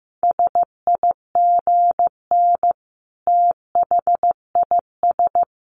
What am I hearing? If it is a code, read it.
Morse code, heard as a keyed tone, about 15 wpm, 700 Hz. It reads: SIGN THIS
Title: A Sound Of Thunder 15wpm